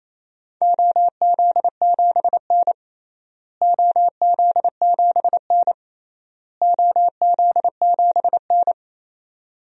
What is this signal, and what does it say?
Morse code, heard as a keyed tone, about 28 wpm, 700 Hz. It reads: OZ7N OZ7N OZ7N
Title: DXCC-100 Call Signs - 4 Characters - Repeated 3x - 28wpm